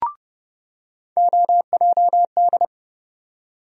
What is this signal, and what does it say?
Morse code, heard as a keyed tone, about 30 wpm, 700 Hz. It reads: OJD
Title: USA Call Sign Postfixes - 3 Letters 30wpm